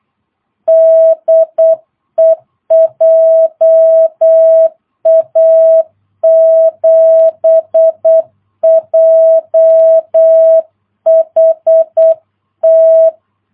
Morse code, heard as a keyed tone, about 8 wpm, 660 Hz. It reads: DE JA7JHT
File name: jh8cbh / 2022cw